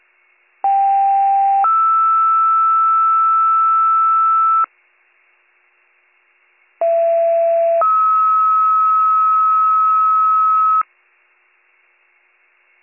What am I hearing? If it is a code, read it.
Two-tone page, heard as a one-second tone, then a three-second tone, about 790 then 1320 Hz, about 670 then 1290 Hz. It is heard from 0.6 to 10.8 s.